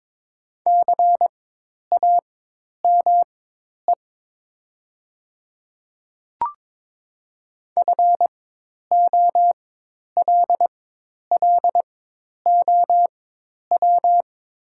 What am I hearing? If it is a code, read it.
Morse code, heard as a keyed tone, about 22 wpm, spaced as for 10 wpm, 700 Hz. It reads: CAME FOLLOW
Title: 4x Character Spacing - Top 200 Words - 22wpm